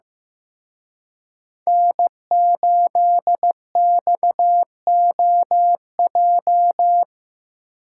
Morse code, heard as a keyed tone, about 18 wpm, 700 Hz. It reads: N8XOJ
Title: Speed Racing - USA 1x3 Call Signs 15wpm